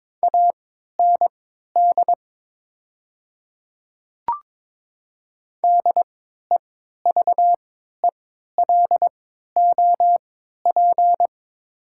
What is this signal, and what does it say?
Morse code, heard as a keyed tone, about 22 wpm, spaced as for 13 wpm, 700 Hz. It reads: AND DEVELOP
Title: 3x Character Spacing - Top 500 Words - 22wpm